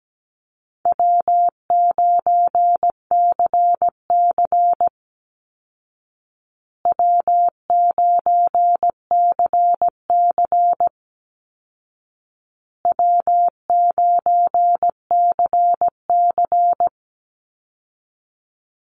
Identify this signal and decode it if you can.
Morse code, heard as a keyed tone, about 17 wpm, 700 Hz. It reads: W9CC W9CC W9CC
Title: Call Signs Repeated 2x - 17wpm